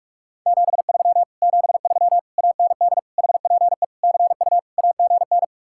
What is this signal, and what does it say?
Morse code, heard as a keyed tone, about 44 wpm, 700 Hz. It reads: 73 73 AND HPE CU AGN